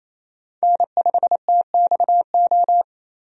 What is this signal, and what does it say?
Morse code, heard as a keyed tone, about 28 wpm, 700 Hz. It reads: N5TXO
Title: USA Call Signs - 5 Characters - 28wpm